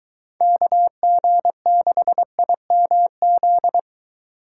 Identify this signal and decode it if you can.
Morse code, heard as a keyed tone, about 22 wpm, 700 Hz. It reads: KG6IMZ